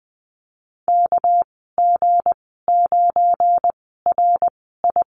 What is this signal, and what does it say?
Morse code, heard as a keyed tone, about 20 wpm, spaced as for 14 wpm, 700 Hz. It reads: KG9RI
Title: 245 Lesson 39g - Letter z - Call Signs 20wpm